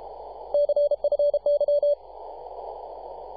Morse code, heard as a keyed tone, about 33 wpm, 580 Hz. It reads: CFY
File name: posts / 2015-sept-24-2118z-t2gc-18-071-cw